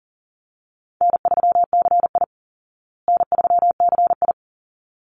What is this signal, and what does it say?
Morse code, heard as a keyed tone, about 40 wpm, 700 Hz. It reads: N3CI N3CI
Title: Easy Call Signs Repeated 2x - 40wpm